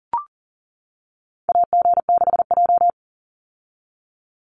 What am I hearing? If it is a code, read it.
Morse code, heard as a keyed tone, about 40 wpm, 700 Hz. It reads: AG6J
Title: USA Call Signs 40wpm